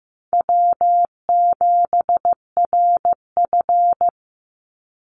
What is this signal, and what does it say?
Morse code, heard as a keyed tone, about 15 wpm, 700 Hz. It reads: W7RF
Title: Easy Call Signs Repeated 3x - 15wpm